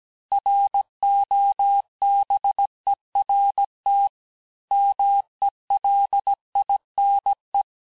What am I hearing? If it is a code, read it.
Morse code, heard as a keyed tone, about 17 wpm, 800 Hz. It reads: ROBERT MELINE
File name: cours-du-lundi files / cours-1587-FIN-de-QSO-vitesse-16MPM